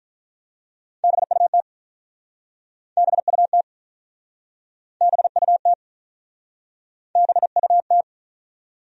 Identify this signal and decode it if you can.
Morse code, heard as a keyed tone, about 48 wpm, 700 Hz. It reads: BUT BUT BUT BUT